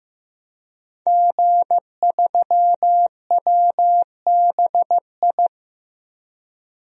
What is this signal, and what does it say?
Morse code, heard as a keyed tone, about 15 wpm, 700 Hz. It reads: G3WBI
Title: DXCC-100 Call Signs - 5 Characters - Repeated 3x - 15wpm